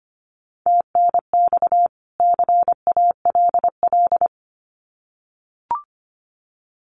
Morse code, heard as a keyed tone, about 25 wpm, 700 Hz. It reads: TNX CALL
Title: Common QSO Phrases 25wpm